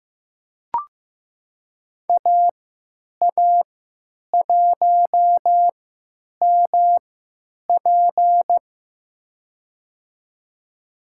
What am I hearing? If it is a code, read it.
Morse code, heard as a keyed tone, about 15 wpm, spaced as for 9 wpm, 700 Hz. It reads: AA1MP
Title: USA Call Signs - 5 Characters - 3x Character Spacing - 15wpm